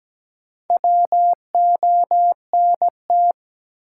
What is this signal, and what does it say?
Morse code, heard as a keyed tone, about 17 wpm, 700 Hz. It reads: WONT
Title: Sentences from Top 1000 Words - Review Part 6 - New Word Drill - 17wpm